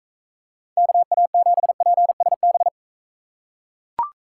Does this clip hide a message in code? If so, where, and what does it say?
Yes: Morse code from 0.8 to 2.7 s, heard as a keyed tone, about 43 wpm, 700 Hz. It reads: KA7PSB